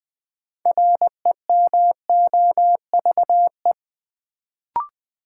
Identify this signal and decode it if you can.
Morse code, heard as a keyed tone, about 20 wpm, 700 Hz. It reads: REMOVE
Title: Sentences from Top 800 Words - New Word Drill - 20wpm